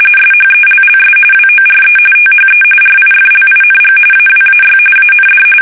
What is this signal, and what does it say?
RTTY, about 45.45 baud. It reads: SHELLS WORE THE RIFLING AWAY AND
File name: fsk850 / battleshipnj